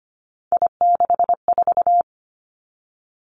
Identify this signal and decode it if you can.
Morse code, heard as a keyed tone, about 25 wpm, 700 Hz. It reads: I64
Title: Three Challenging Letter-Number Combinations 25wpm